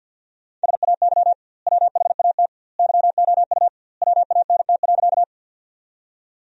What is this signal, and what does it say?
Morse code, heard as a keyed tone, about 50 wpm, 700 Hz. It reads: SAY WHAT YOU WANT.
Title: 2x Word-Spacing - Sentences from Top 200 Words - 50wpm